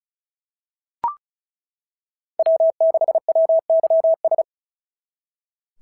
Morse code, heard as a keyed tone, about 34 wpm, 690 Hz. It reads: W6WYS
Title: SOTA QSO Practice 35wpm